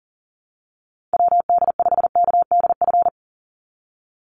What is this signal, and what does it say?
Morse code, heard as a keyed tone, about 40 wpm, 700 Hz. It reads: WD5KDF
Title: USA Call Signs 40wpm